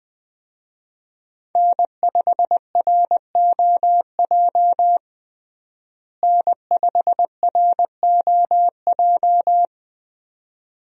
Morse code, heard as a keyed tone, about 20 wpm, 700 Hz. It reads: N5ROJ N5ROJ
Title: USA Call Signs - 5 Characters - Repeated 2x - 20wpm